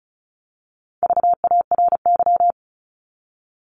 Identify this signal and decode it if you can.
Morse code, heard as a keyed tone, about 34 wpm, 700 Hz. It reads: VARY
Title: Top 1001-1250 Words 35wpm